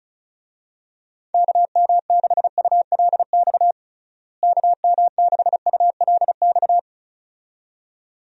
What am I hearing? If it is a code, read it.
Morse code, heard as a keyed tone, about 34 wpm, 700 Hz. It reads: KM6ULX KM6ULX
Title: USA Call Signs - 6 Characters - Repeated 2x - 35wpm